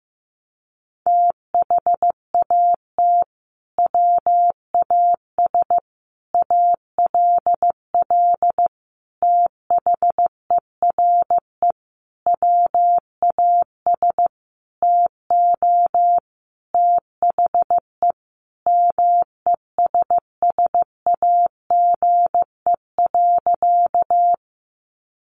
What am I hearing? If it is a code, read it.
Morse code, heard as a keyed tone, about 14 wpm, 700 Hz. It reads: THAT WAS ALL THERE WAS TO THE MESSAGE.
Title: Sentences from Top 1100 Words 15wpm